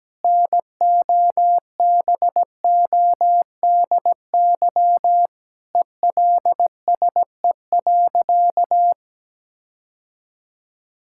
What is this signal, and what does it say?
Morse code, heard as a keyed tone, about 17 wpm, 700 Hz. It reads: NOBODY ELSE.
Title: Sentences from Top 1600 Words 17wpm